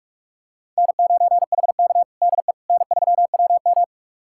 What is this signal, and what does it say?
Morse code, heard as a keyed tone, about 41 wpm, 700 Hz. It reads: N9HK DE N3WM
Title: USA Callsign DE Callsign - 4 Characters - 45wpm